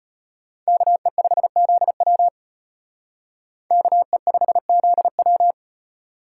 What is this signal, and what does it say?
Morse code, heard as a keyed tone, about 33 wpm, 700 Hz. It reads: KE5ZW KE5ZW